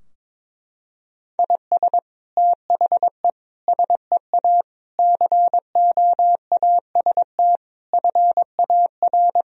Morse code, heard as a keyed tone, about 22 wpm, 700 Hz. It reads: IS THE SEA COAST FAR
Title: Sentences from Top 1000 Words - Review Part 4 - 22wpm